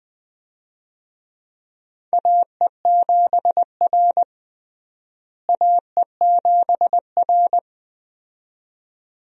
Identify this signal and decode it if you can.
Morse code, heard as a keyed tone, about 20 wpm, 700 Hz. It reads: AE7R AE7R
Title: USA Call Signs - 4 Characters - Repeated 2x - 20wpm